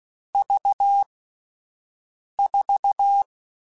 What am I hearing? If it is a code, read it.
Morse code, heard as a keyed tone, about 16 wpm, 800 Hz. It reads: V 4
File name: Difficult Combos / Difficult Combos-mp3s / AUV AUV4 @10